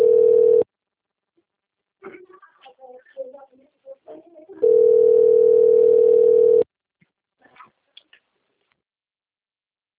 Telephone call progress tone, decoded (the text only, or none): ringback tone